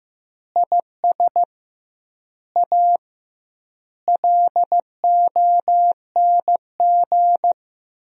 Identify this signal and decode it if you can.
Morse code, heard as a keyed tone, about 15 wpm, 700 Hz. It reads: IS A LONG
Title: 2x Word-Spacing - Sentences from Top 100 Words - 15wpm